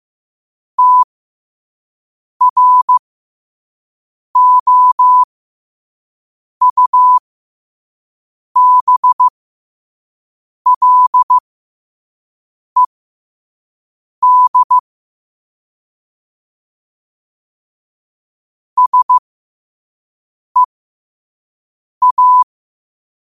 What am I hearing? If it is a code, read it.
Morse code, heard as a keyed tone, about 15 wpm, spaced as for 5 wpm, 1000 Hz. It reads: TROUBLED SEA
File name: 2026-03-16-5wpm